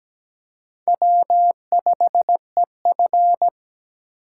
Morse code, heard as a keyed tone, about 17 wpm, 700 Hz. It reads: W5EF
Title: USA Call Signs - 4 Characters - 8x Character Spacing - 17wpm